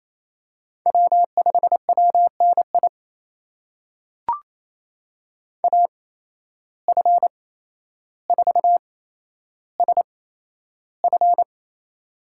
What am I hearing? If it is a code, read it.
Morse code, heard as a keyed tone, about 28 wpm, spaced as for 12 wpm, 700 Hz. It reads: W5WNI AF4SF
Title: USA Call Signs - 5 Characters - 8x Character Spacing - 28wpm